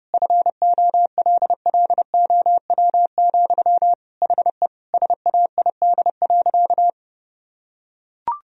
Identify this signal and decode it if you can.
Morse code, heard as a keyed tone, about 31 wpm, 700 Hz. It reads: FOLLOW, HE SAID.
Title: Sentences from Top 1000 Words - Review Part 1 - 30wpm